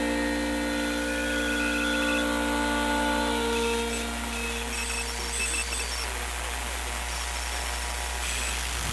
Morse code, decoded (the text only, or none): none